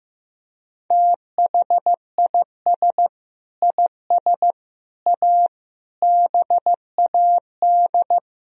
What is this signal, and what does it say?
Morse code, heard as a keyed tone, about 15 wpm, 700 Hz. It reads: THIS IS A BAD
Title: Sentences from Top 1000 Words - Review Part 1 - 15wpm